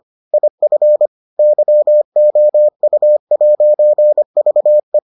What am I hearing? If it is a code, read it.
Morse code, heard as a keyed tone, about 25 wpm, 600 Hz. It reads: IF YOU'VE